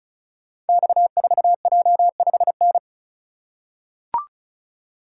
Morse code, heard as a keyed tone, about 36 wpm, 700 Hz. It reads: X4J5N
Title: Code Groups - 5 Letter:Numbers - 35wpm